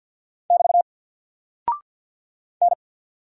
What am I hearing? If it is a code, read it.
Morse code, heard as a keyed tone, about 50 wpm, 700 Hz. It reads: <BT> N
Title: Single Character (Letter:Number:Punctuation:Prosign) 50wpm